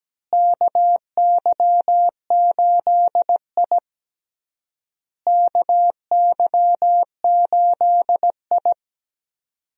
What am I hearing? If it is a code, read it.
Morse code, heard as a keyed tone, about 17 wpm, 700 Hz. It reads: KY8I KY8I
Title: USA Call Signs - 4 Characters - Repeated 3x - 17wpm